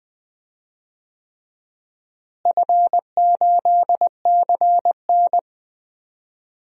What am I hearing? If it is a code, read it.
Morse code, heard as a keyed tone, about 20 wpm, 700 Hz. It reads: F8CN
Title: DXCC-100 Call Signs - 4 Characters - Repeated 3x - 20wpm